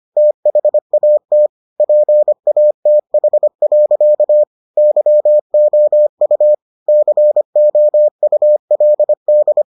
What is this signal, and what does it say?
Morse code, heard as a keyed tone, about 25 wpm, 600 Hz. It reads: THAT PATH. YOU COULD